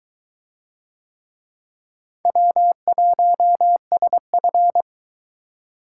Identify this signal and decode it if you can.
Morse code, heard as a keyed tone, about 20 wpm, 700 Hz. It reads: W1SF